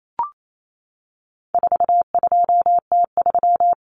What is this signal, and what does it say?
Morse code, heard as a keyed tone, about 28 wpm, 700 Hz. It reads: 42T3
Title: Common Cut Numbers - 4-Digit 28wpm